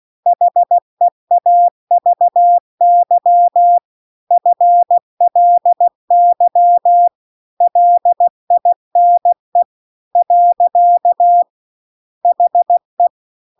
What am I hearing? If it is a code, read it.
Morse code, heard as a keyed tone, about 16 wpm, 700 Hz. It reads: HEAVY FLY LINE . HE
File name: CW-The-Big-Twoheaded-River-2-16WPM